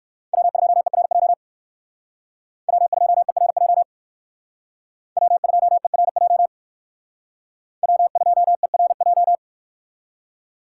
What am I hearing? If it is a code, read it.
Morse code, heard as a keyed tone, about 61 wpm, 700 Hz. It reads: W2ERJ W2ERJ W2ERJ W2ERJ